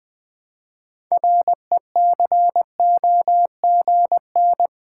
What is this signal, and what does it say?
Morse code, heard as a keyed tone, about 20 wpm, 700 Hz. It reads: RECOGN
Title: Build A Word - 3 Letters At A Time - 9-Letter Words - 20wpm